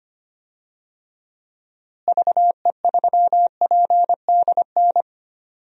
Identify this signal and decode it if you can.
Morse code, heard as a keyed tone, about 25 wpm, 700 Hz. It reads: VE3PDN
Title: DXCC-100 Call Signs - 6 Characters - Repeated 3x - 25wpm